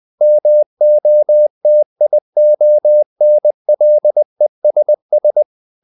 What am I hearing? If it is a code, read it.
Morse code, heard as a keyed tone, about 20 wpm, 600 Hz. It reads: MOTIONLESS
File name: the old man and the sea part 2 20wpm